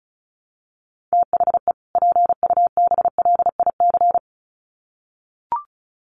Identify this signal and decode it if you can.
Morse code, heard as a keyed tone, about 34 wpm, 700 Hz. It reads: THE PUBLIC
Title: Sets of 2 Words 35wpm